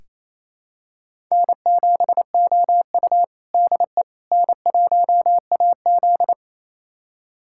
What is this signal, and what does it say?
Morse code, heard as a keyed tone, about 28 wpm, 700 Hz. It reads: N7OU DE N1AZ
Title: USA Callsign DE Callsign - 4 Characters - 28wpm